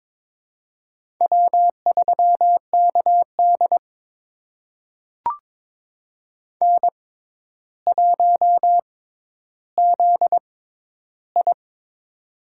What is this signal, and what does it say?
Morse code, heard as a keyed tone, about 22 wpm, spaced as for 11 wpm, 700 Hz. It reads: W3KD N1ZI
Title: USA Call Signs - 4 Characters - 6x Character Spacing - 22wpm